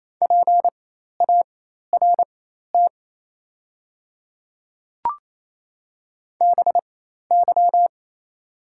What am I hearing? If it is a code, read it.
Morse code, heard as a keyed tone, about 28 wpm, spaced as for 13 wpm, 700 Hz. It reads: PART BY
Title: 4x Character Spacing - Top 200 Words - 28wpm